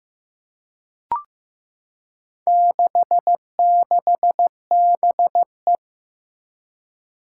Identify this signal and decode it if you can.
Morse code, heard as a keyed tone, about 15 wpm, 700 Hz. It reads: 66BE
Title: Four Challenging Letter-Number Combinations 15wpm